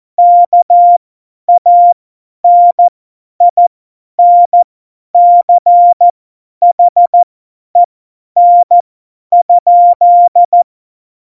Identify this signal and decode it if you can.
Morse code, heard as a keyed tone, about 14 wpm, spaced as for 10 wpm, 700 Hz. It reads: KANINCHEN?
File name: Morsekurs / Witze / Witz 22. Januar 2019, 50 F70